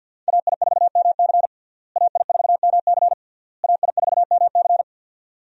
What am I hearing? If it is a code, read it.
Morse code, heard as a keyed tone, about 52 wpm, 700 Hz. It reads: AI4MC AI4MC AI4MC
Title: USA Call Signs - 5 Characters - Repeated 3x - 50wpm